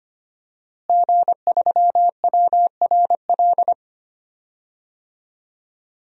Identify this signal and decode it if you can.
Morse code, heard as a keyed tone, about 25 wpm, 700 Hz. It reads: G3WRL
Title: DXCC-100 Call Signs - 5 Characters - Repeated 2x - 25wpm